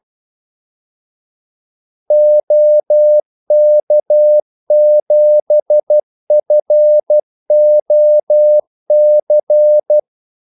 Morse code, heard as a keyed tone, about 12 wpm, 600 Hz. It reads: OK7FOC
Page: 2 ekstra ordmellemrum mellem kaldesignalerne